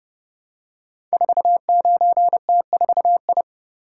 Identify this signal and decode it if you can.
Morse code, heard as a keyed tone, about 30 wpm, 700 Hz. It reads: 49T4I
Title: Code Groups - 5 Letter:Numbers - 30wpm